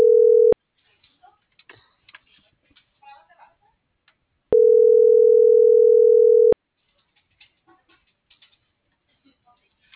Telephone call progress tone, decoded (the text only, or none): ringback tone